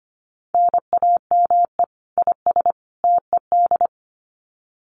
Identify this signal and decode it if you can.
Morse code, heard as a keyed tone, about 25 wpm, 700 Hz. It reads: NAME IS TED
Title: Common QSO Phrases 25wpm